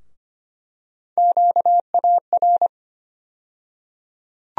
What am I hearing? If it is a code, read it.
Morse code, heard as a keyed tone, about 25 wpm, 700 Hz. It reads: QAR
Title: USA Call Sign Postfixes - 3 Letters 25wpm